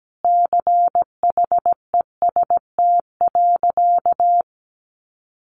Morse code, heard as a keyed tone, about 17 wpm, 700 Hz. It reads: CHEST.
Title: A Sound Of Thunder 17wpm